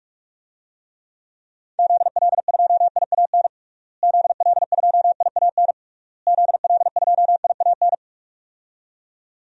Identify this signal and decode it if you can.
Morse code, heard as a keyed tone, about 46 wpm, 700 Hz. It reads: ZL2IAN ZL2IAN ZL2IAN
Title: DXCC-100 Call Signs - 6 Characters - Repeated 3x - 45wpm